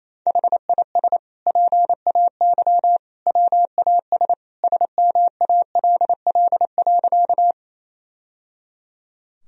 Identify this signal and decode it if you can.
Morse code, heard as a keyed tone, about 28 wpm, 700 Hz. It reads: HIS PAY WAS SMALL.
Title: Sentences from Top 1000 Words - Review Part 1 - 28wpm